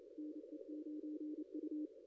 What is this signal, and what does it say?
Morse code, heard as a keyed tone, about 28 wpm, 300 Hz, with radio noise. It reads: NE9U
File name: contests / ss-cw-k5tr-op-n5rz -0200